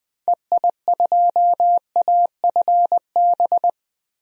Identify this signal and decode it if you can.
Morse code, heard as a keyed tone, about 20 wpm, 700 Hz. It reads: EI2AFB
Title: DXCC-100 Call Signs - 6 Characters - 20wpm